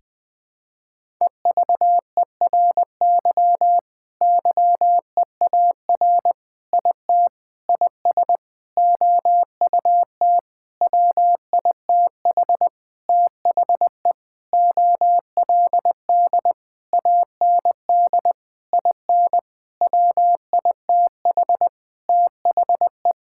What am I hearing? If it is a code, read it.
Morse code, heard as a keyed tone, about 20 wpm, 700 Hz. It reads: EVERY YEAR IT IS OUT WITH THE OLD AND IN WITH THE
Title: Sentences with 9-Word Phrases - 20wpm